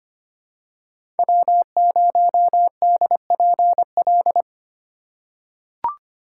Morse code, heard as a keyed tone, about 25 wpm, 700 Hz. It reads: W0DPL
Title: USA Call Signs - 5 Characters - 4x Character Spacing - 25wpm